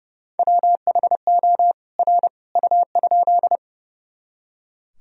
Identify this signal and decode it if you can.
Morse code, heard as a keyed tone, about 30 wpm, 700 Hz. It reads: WHO R U?
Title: Sentences from Top 100 Words and Abbreviations 30wpm